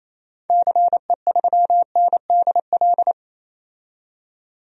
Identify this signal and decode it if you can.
Morse code, heard as a keyed tone, about 28 wpm, 700 Hz. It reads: CE3NDL
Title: DXCC-100 Call Signs - 6 Characters - 28wpm